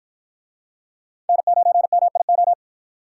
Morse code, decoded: N9MIO

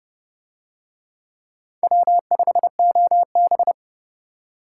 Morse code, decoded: W5OB

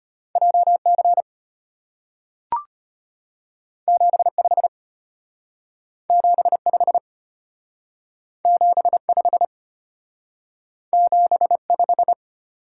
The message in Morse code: JC 75 75 75 75

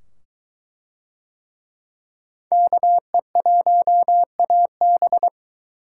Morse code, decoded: KE1AB

